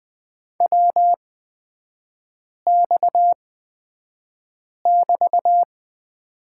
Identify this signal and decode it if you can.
Morse code, heard as a keyed tone, about 20 wpm, spaced as for 5 wpm, 690 Hz. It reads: WX<BT>
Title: 229 Lesson 37e - Letter x - Three Characters 20wpm